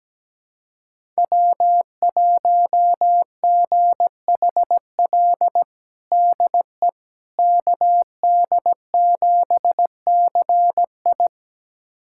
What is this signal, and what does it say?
Morse code, heard as a keyed tone, about 17 wpm, 700 Hz. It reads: W1GHL DE KD7CI
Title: USA Callsign DE Callsign - 5 Characters - 17wpm